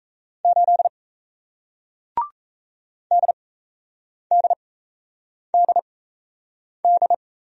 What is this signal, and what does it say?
Morse code, heard as a keyed tone, about 37 wpm, spaced as for 8 wpm, 700 Hz. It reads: 8 DDDD